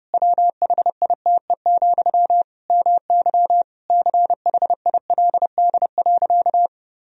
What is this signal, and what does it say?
Morse code, heard as a keyed tone, about 30 wpm, 700 Hz. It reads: WHITE, MY CHILD.
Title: Sentences from Top 1000 Words - Review Part 1 - 30wpm